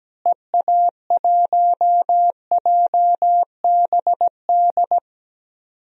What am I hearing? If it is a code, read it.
Morse code, heard as a keyed tone, about 17 wpm, 700 Hz. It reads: EA1JBD
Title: DXCC-100 Call Signs - 6 Characters - Repeated 2x - 17wpm